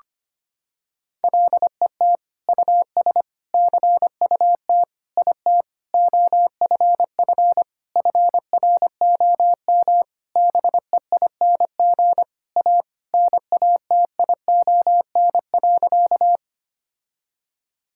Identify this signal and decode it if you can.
Morse code, heard as a keyed tone, about 25 wpm, 700 Hz. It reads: LET US CUT IT OFF FROM BEING A NATION.
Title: Sentences from Top 1000 Words - Review Part 6 - 25wpm